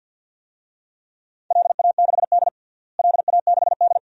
Morse code, heard as a keyed tone, about 46 wpm, 700 Hz. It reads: LA6D LA6D
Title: DXCC-100 Call Signs - 4 Characters - Repeated 2x - 50wpm